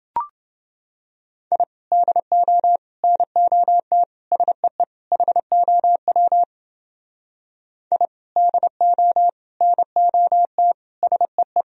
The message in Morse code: I DO NOT SEE HOW I DO NOT SEE

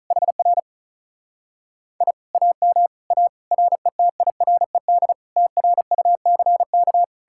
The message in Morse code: HR I AM A RETIRED TRUCK

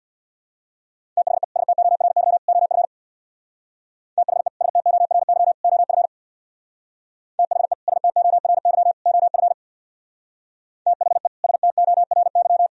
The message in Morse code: THE STORY OF THE STORY OF THE STORY OF THE STORY

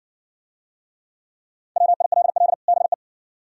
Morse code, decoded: WILL BE

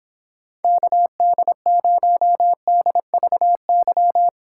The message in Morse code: KD0DVY

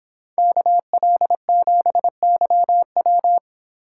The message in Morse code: KL7YW

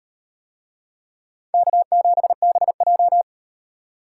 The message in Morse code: K7BJ

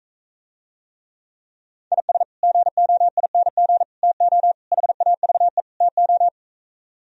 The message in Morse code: IS GOING TO HAVE TO

